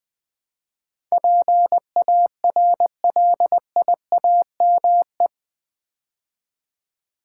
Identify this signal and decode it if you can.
Morse code, heard as a keyed tone, about 20 wpm, 700 Hz. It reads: PARLIAME
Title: Build A Word - 4 Letters At A Time - 10-Letter Words - 20wpm